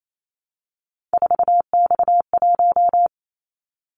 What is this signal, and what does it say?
Morse code, heard as a keyed tone, about 28 wpm, 700 Hz. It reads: 4X1
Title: Callsign Prefixes for Top 50 DXCC - 28wpm